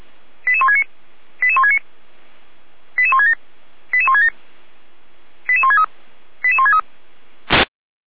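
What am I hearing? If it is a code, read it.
Five-tone selective call (ZVEI), heard as quick runs of five tones, tones of about 70 ms. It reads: 88169 88169 88167 88167 88163 88163
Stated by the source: ZVEI five-tone selective call 88352 received by SDR.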